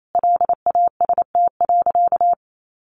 Morse code, heard as a keyed tone, about 28 wpm, 700 Hz. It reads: LAST.
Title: Sentences from Top 1400 Words 28wpm